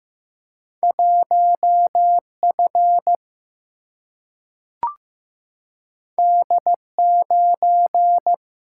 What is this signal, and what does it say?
Morse code, heard as a keyed tone, about 15 wpm, 700 Hz. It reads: 1 F D 9
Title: Single-Letter Single-Number 15wpm